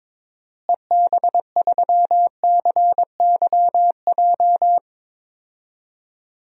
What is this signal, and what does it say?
Morse code, heard as a keyed tone, about 22 wpm, 700 Hz. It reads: EB3CYJ
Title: DXCC-100 Call Signs - 6 Characters - Repeated 2x - 22wpm